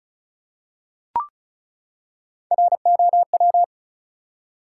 Morse code, heard as a keyed tone, about 34 wpm, 700 Hz. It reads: ROW
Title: Three Letter Words - 35wpm